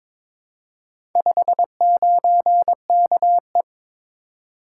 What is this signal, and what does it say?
Morse code, heard as a keyed tone, about 22 wpm, 700 Hz. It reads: 59KE
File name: practice / 22wpm / Four Letter-Number Combinations - 22wpm